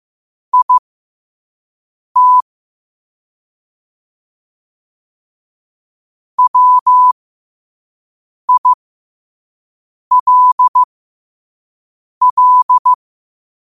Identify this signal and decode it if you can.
Morse code, heard as a keyed tone, about 15 wpm, spaced as for 5 wpm, 1000 Hz. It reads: IT WILL
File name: qotdfiles / 5wpm